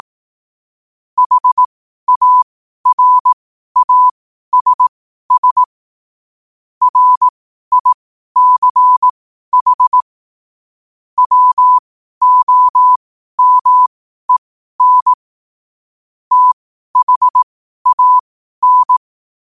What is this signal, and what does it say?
Morse code, heard as a keyed tone, about 18 wpm, spaced as for 12 wpm, 1000 Hz. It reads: HARASS RICH WOMEN THAN
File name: qotdfiles / 12wpm